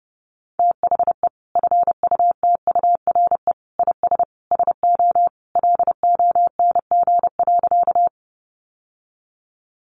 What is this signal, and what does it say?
Morse code, heard as a keyed tone, about 28 wpm, 700 Hz. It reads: THE FUTURE IS SO LONG.
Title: Sentences from Top 700 Words 30wpm